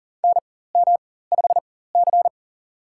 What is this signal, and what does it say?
Morse code, heard as a keyed tone, about 40 wpm, spaced as for 19 wpm, 700 Hz. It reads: NM5C
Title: USA Call Signs - 4x Character Spacing - 40wpm